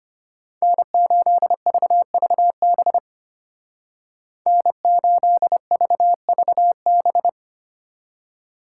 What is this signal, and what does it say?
Morse code, heard as a keyed tone, about 34 wpm, 700 Hz. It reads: N8VVB N8VVB